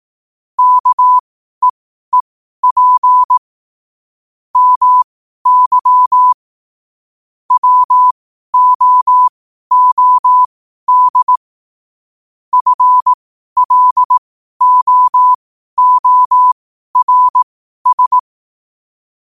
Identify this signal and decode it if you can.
Morse code, heard as a keyed tone, about 18 wpm, spaced as for 12 wpm, 1000 Hz. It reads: KEEP MY WOOD FLOORS
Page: New quotes every day in morse code at 12 Words per minute.